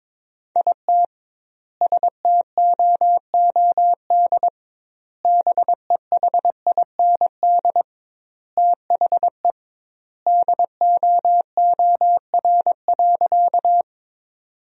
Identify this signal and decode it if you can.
Morse code, heard as a keyed tone, about 22 wpm, 700 Hz. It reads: IT STOOD BEHIND THE DOOR.
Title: 2x Word-Spacing - Sentences from Top 500 Words - 22wpm